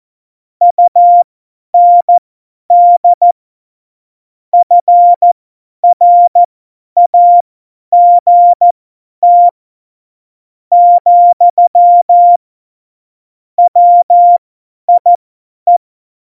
Morse code, heard as a keyed tone, about 14 wpm, spaced as for 10 wpm, 700 Hz. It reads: UND FRAGT , WIE
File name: Morsekurs / Witze / Witz 16. Oktober 2018, 50 F70